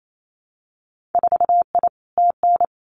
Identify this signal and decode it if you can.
Morse code, heard as a keyed tone, about 28 wpm, 700 Hz. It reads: 4I TN
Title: Winter Field Day Exchanges 28wpm